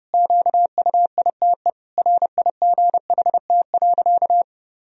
Morse code, heard as a keyed tone, about 30 wpm, 700 Hz. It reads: QUITE RIGHT.
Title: Sentences from Top 1000 Words - Review Part 2 - 30wpm